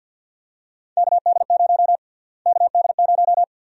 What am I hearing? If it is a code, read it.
Morse code, heard as a keyed tone, about 53 wpm, 700 Hz. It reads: KD0 KD0
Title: USA Call Sign Prefixes - Repeated 2x - 50wpm